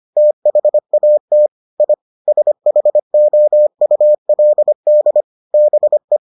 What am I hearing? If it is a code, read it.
Morse code, heard as a keyed tone, about 25 wpm, 600 Hz. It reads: THAT I SHOULD BE